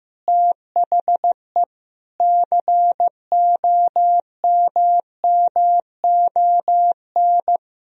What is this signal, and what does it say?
Morse code, heard as a keyed tone, about 15 wpm, 700 Hz. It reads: THE COMMON
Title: Sentences from Top 1000 Words - Review Part 3 - 15wpm